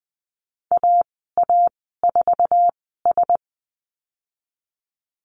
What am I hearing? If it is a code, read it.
Morse code, heard as a keyed tone, about 20 wpm, spaced as for 15 wpm, 700 Hz. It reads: AA4S
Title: 055 Lesson 10g - Letter r - Call Signs 20wpm